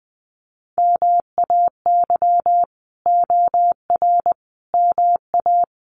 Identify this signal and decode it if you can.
Morse code, heard as a keyed tone, about 20 wpm, 700 Hz. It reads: MAY OR MA
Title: Sets of 5 Words 20wpm